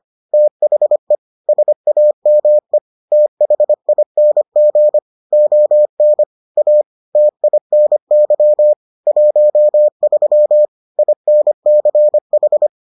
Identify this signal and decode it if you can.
Morse code, heard as a keyed tone, about 25 wpm, 600 Hz. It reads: THE SAME THING ON A TINY 13 INCH